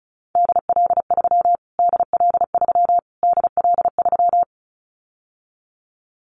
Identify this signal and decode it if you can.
Morse code, heard as a keyed tone, about 34 wpm, 700 Hz. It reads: DL3 DL3 DL3
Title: Callsign Prefixes for Top 25 DXCC - Repeated 3x - 35wpm